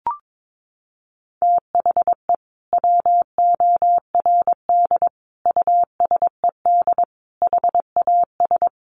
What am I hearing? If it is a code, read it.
Morse code, heard as a keyed tone, about 22 wpm, 700 Hz. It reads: THE WORD USED HAS
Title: Sentences from Top 1100 Words 22wpm